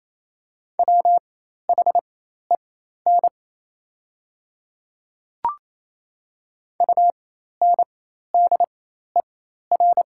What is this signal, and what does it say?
Morse code, heard as a keyed tone, about 28 wpm, spaced as for 13 wpm, 700 Hz. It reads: WHEN UNDER
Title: 4x Character Spacing - Top 200 Words - 28wpm